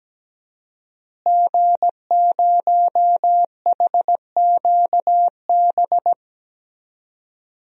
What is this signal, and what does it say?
Morse code, heard as a keyed tone, about 17 wpm, 700 Hz. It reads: G0HQB
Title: DXCC-100 Call Signs - 5 Characters - Repeated 3x - 17wpm